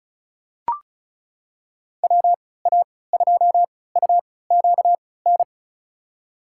Morse code, heard as a keyed tone, about 34 wpm, spaced as for 20 wpm, 700 Hz. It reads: WA2UQN
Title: USA Call Signs - 6 Characters - 3x Character Spacing - 35wpm